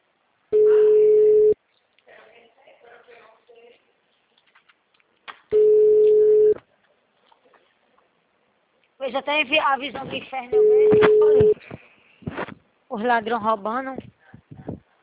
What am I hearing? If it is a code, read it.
Ringback tone (Europe, 425 Hz), with speech.